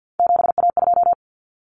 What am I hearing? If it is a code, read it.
Morse code, heard as a keyed tone, about 50 wpm, 700 Hz. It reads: 7A2
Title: Common Cut Numbers - 3-Digit 50wpm